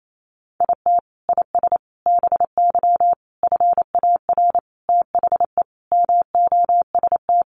Morse code, heard as a keyed tone, about 28 wpm, 700 Hz. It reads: IT IS BY FAR THE MOST
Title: Sentences from Top 1200 Words 28wpm